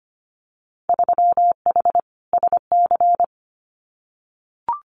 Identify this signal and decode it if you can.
Morse code, heard as a keyed tone, about 25 wpm, 700 Hz. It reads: 3H SC
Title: Winter Field Day Exchanges 25wpm